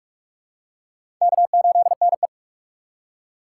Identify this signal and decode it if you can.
Morse code, heard as a keyed tone, about 44 wpm, 700 Hz. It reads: K8NE
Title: USA Call Signs - 8x Character Spacing - 45wpm